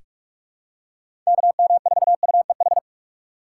Morse code, heard as a keyed tone, about 46 wpm, 700 Hz. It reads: KM4UEH
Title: USA Call Signs - 4x Character Spacing - 45wpm